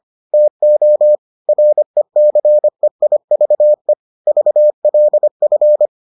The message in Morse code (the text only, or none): TO RECEIVE VLF